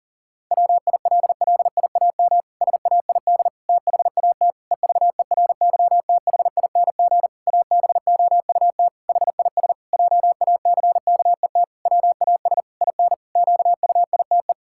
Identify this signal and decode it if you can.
Morse code, heard as a keyed tone, about 40 wpm, 700 Hz. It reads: WILLIAM SAID THAT EVERYTHING ABOUT HIS JACKET WAS IN QUITE
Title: Pangram Challenge v2 - 40wpm